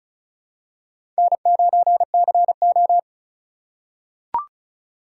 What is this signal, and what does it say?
Morse code, heard as a keyed tone, about 34 wpm, 700 Hz. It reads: N9CO
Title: USA Call Signs - 4 Characters - 2x Character Spacing - 35wpm